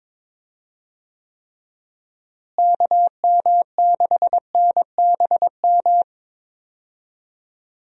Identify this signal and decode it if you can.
Morse code, heard as a keyed tone, about 22 wpm, 700 Hz. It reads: KM6NBM
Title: USA Call Signs - 6 Characters - Repeated 2x - 22wpm